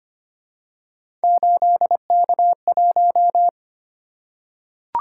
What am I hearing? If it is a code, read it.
Morse code, heard as a keyed tone, about 25 wpm, 700 Hz. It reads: 8K1
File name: Three Letter-Number Combinations - 25wpm